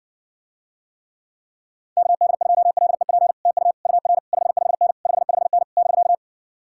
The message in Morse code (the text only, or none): KD2LEP TU UR 55N 55N <BK>